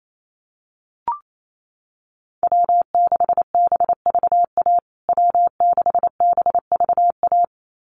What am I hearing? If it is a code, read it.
Morse code, heard as a keyed tone, about 28 wpm, 700 Hz. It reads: W6BVA W6BVA
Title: USA Call Signs Repeated 2x 28wpm